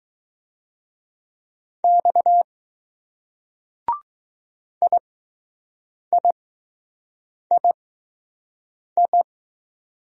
Morse code, heard as a keyed tone, about 20 wpm, spaced as for 6 wpm, 700 Hz. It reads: X IIII